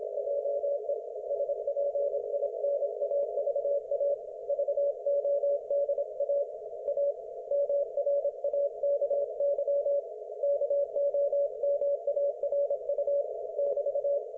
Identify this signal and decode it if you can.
Morse code, heard as a keyed tone, about 27 wpm, 570 Hz. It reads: JE SPINAVA VODA A MRAKY KOMARU <BT>
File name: Archiv / CUC216 / ok1fhi cul216